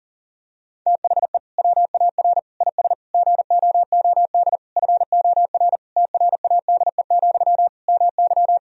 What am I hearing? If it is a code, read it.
Morse code, heard as a keyed tone, about 37 wpm, 700 Hz. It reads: THE WAR IS GOOD FOR TRADE, MY
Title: Sentences from Top 1000 Words - Review Part 4 - 40wpm